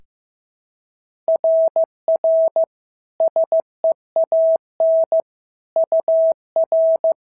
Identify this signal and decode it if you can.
Morse code, heard as a keyed tone, about 15 wpm, 650 Hz. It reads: RR SEAN UR
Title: SOTA QSO Practice 15wpm